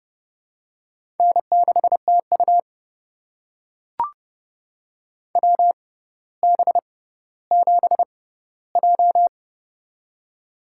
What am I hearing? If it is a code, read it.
Morse code, heard as a keyed tone, about 30 wpm, spaced as for 15 wpm, 690 Hz. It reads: N6TU WB7J
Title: USA Call Signs - 4 Characters - 6x Character Spacing - 30wpm